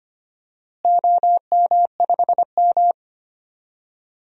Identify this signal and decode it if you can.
Morse code, heard as a keyed tone, about 25 wpm, 700 Hz. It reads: OM5M
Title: DXCC-100 Call Signs - 4 Characters - Repeated 2x - 25wpm